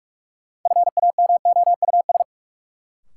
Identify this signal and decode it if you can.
Morse code, heard as a keyed tone, about 43 wpm, 700 Hz. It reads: FAMOUS
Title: Sentences from Top 1500 Words - New Word Drill - 45wpm